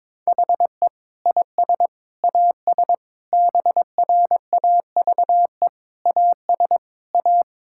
Morse code, heard as a keyed tone, about 22 wpm, 700 Hz. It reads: HE IS AS BRAVE AS A
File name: Difficult Sentences with Hints - 22wpm